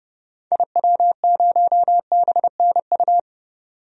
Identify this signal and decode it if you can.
Morse code, heard as a keyed tone, about 30 wpm, 700 Hz. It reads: IW0BNU
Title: DXCC-100 Call Signs - 6 Characters - 30wpm